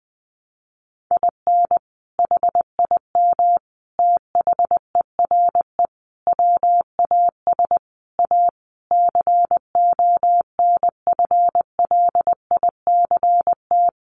Morse code, heard as a keyed tone, about 20 wpm, 700 Hz. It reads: IN HIM THERE WAS A CONFLICT
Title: Sentences from Top 1300 Words 20wpm